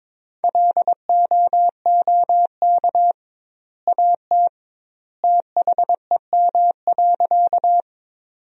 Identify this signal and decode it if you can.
Morse code, heard as a keyed tone, about 22 wpm, 700 Hz. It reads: LOOK AT THEM.
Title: 2x Word-Spacing - Sentences from Top 100 Words - 22wpm